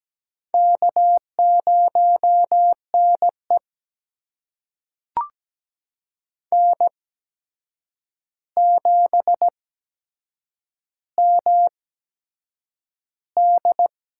Morse code, heard as a keyed tone, about 17 wpm, spaced as for 7 wpm, 700 Hz. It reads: K0NE N7MD
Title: USA Call Signs - 4 Characters - 8x Character Spacing - 17wpm